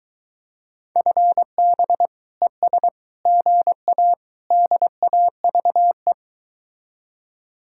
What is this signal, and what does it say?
Morse code, heard as a keyed tone, about 23 wpm, 700 Hz. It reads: FB ES GA DAVE